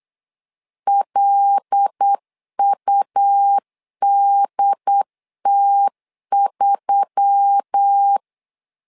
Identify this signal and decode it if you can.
Morse code, heard as a keyed tone, about 8 wpm, 800 Hz. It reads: LUDT3
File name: LEZIONI / 5 LBVC / 5 L 8wpm 2831